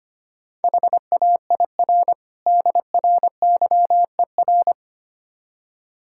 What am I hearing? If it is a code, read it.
Morse code, heard as a keyed tone, about 28 wpm, 700 Hz. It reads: HAIR DRYER
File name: practice / 25wpm / Difficult Words - Speed-Racing with Hints - 25wpm